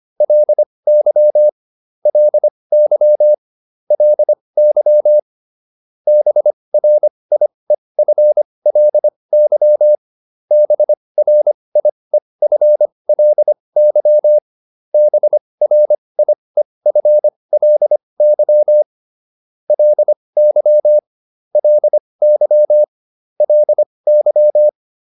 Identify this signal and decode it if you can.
Morse code, heard as a keyed tone, about 25 wpm, spaced as for 20 wpm, 600 Hz. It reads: LY LY LY BRIEFLY BRIEFLY BRIEFLY LY LY LY